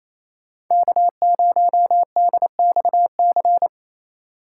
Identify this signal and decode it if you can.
Morse code, heard as a keyed tone, about 28 wpm, 700 Hz. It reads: K0DXC
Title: K1USN SST QSO Practice - 28wpm